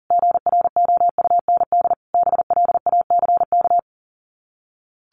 Morse code, heard as a keyed tone, about 40 wpm, 700 Hz. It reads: GROUND BLACK
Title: Sets of 5 Words Encore 40wpm